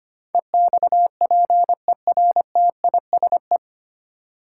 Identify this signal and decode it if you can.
Morse code, heard as a keyed tone, about 25 wpm, 700 Hz. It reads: EXPERTISE
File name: 9 Letter Words - First 3 Letters Given - 25wpm